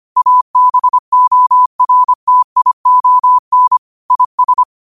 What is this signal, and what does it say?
Morse code, heard as a keyed tone, about 25 wpm, 1000 Hz. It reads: ABORTION IS